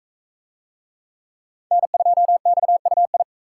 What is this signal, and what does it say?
Morse code, heard as a keyed tone, about 44 wpm, 700 Hz. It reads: N2XUI